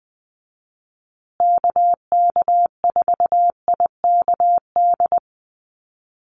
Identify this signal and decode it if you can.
Morse code, heard as a keyed tone, about 20 wpm, 700 Hz. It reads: KK4IKD
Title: USA Call Signs Repeated 2x 20wpm